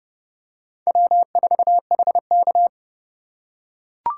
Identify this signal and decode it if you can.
Morse code, heard as a keyed tone, about 32 wpm, 700 Hz. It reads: W4HK
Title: USA Call Signs - 4 Characters - 2x Character Spacing - 30wpm